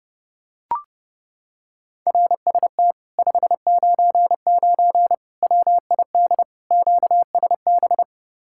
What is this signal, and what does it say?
Morse code, heard as a keyed tone, about 30 wpm, 700 Hz. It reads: RST 599 WID QSB